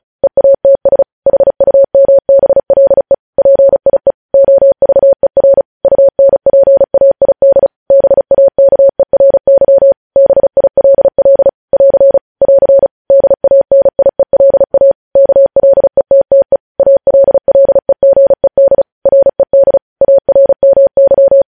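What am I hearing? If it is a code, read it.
Morse code, heard as a keyed tone, about 36 wpm, 550 Hz. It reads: EATS HUMBLE PIE OVER UNPAID BAKERY BILL <AR> <AR> DANIELA KLETTE ALLEGED RED ARMY
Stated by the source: Morse code news headlines at 35.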